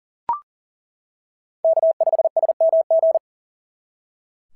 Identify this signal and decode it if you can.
Morse code, heard as a keyed tone, about 38 wpm, 690 Hz. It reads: K5SMG
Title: SOTA QSO Practice 40wpm